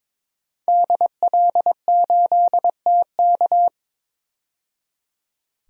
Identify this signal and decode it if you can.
Morse code, heard as a keyed tone, about 22 wpm, 700 Hz. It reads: DL8TK
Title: DXCC-100 Call Signs - 5 Characters - 22wpm